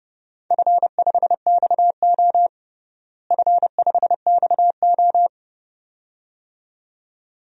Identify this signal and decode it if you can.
Morse code, heard as a keyed tone, about 30 wpm, 700 Hz. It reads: F5XO F5XO
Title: DXCC-100 Call Signs - 4 Characters - Repeated 2x - 30wpm